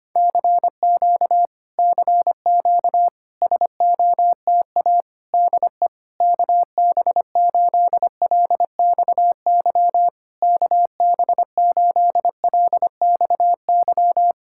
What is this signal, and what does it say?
Morse code, heard as a keyed tone, about 25 wpm, 700 Hz. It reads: CQ CQ SOTA DE KB8LXY KB8LXY
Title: Speed-Racing Common QSO Phrases Encore - 25wpm